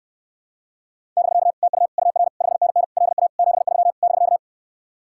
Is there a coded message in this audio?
Yes: Morse code from 1.2 to 4.4 s, heard as a keyed tone, about 69 wpm, 700 Hz. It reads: <BK> TU UR 5NN LA 73 <BK>